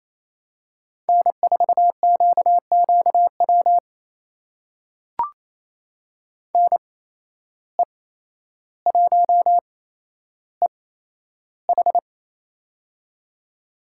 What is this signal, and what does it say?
Morse code, heard as a keyed tone, about 28 wpm, spaced as for 12 wpm, 700 Hz. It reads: N4QQW NE1EH
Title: USA Call Signs - 5 Characters - 8x Character Spacing - 28wpm